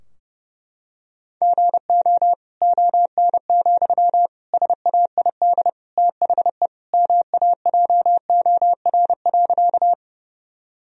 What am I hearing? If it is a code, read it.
Morse code, heard as a keyed tone, about 30 wpm, 700 Hz. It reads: GO ON, SAID THE MAJOR.
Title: Sentences from Top 1000 Words - Review Part 4 - 30wpm